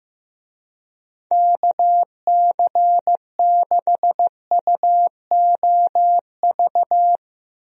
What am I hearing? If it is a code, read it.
Morse code, heard as a keyed tone, about 15 wpm, 700 Hz. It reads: KC6UOV
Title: USA Call Signs - 6 Characters - 15wpm